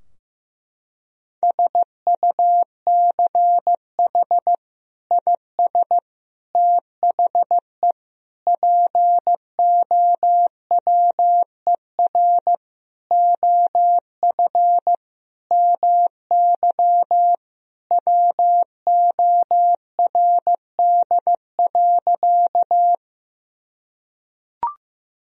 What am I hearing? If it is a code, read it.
Morse code, heard as a keyed tone, about 15 wpm, 700 Hz. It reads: SUCH IS THE POWER OF MY WORD.
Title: Sentences from Top 1000 Words - Review Part 2 - 15wpm